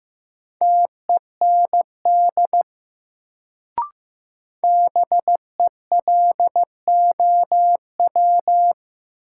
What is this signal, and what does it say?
Morse code, heard as a keyed tone, about 15 wpm, 700 Hz. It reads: TEND BELOW
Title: Sentences from Top 1000 Words - Review Part 5 - New Word Drill - 15wpm